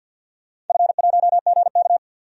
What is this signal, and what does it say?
Morse code, heard as a keyed tone, about 49 wpm, 700 Hz. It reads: F1GK
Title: DXCC-100 Call Signs - 4 Characters - 50wpm